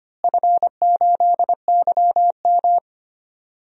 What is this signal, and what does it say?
Morse code, heard as a keyed tone, about 25 wpm, 700 Hz. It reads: F8YM
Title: DXCC-100 Call Signs - 4 Characters - Repeated 3x - 25wpm